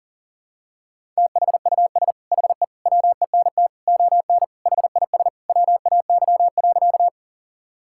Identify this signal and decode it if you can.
Morse code, heard as a keyed tone, about 40 wpm, 700 Hz. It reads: THUS HE WENT ON HIS WAY.
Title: Sentences from Top 1000 Words - Review Part 5 - 40wpm